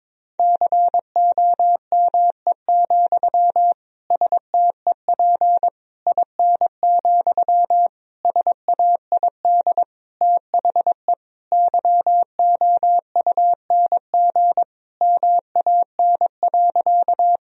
Morse code, heard as a keyed tone, about 22 wpm, 700 Hz. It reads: COME, STEP IN, SAID THE YOUNG MAN.
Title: Sentences from Top 1000 Words - Review Part 2 - 22wpm